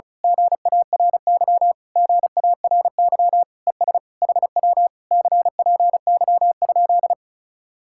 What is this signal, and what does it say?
Morse code, heard as a keyed tone, about 32 wpm, 700 Hz. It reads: GARY GARY ES HW CPY?
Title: Speed-Racing Common QSO Phrases - 35wpm